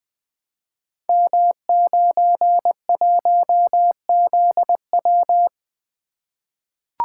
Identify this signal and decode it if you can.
Morse code, heard as a keyed tone, about 20 wpm, 700 Hz. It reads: M91ZW
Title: Code Groups - 5 Letter:Numbers - 20wpm